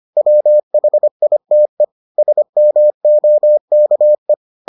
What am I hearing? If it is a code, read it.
Morse code, heard as a keyed tone, about 25 wpm, 600 Hz. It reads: WHITE SMOKE